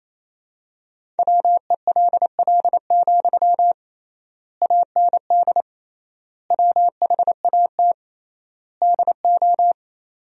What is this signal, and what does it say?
Morse code, heard as a keyed tone, about 28 wpm, 700 Hz. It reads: WELL, AND WHAT DO
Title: 3x Word-Spacing - Sentences from Top 200 Words - 28wpm